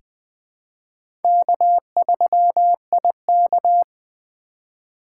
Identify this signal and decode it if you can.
Morse code, heard as a keyed tone, about 20 wpm, 700 Hz. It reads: K3IK
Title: USA Call Signs - 4 Characters - 2x Character Spacing - 20wpm